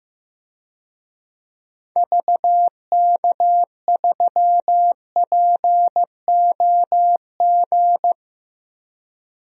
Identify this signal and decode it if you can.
Morse code, heard as a keyed tone, about 15 wpm, 700 Hz. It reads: VK3POG
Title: DXCC-100 Call Signs - 6 Characters - Repeated 2x - 15wpm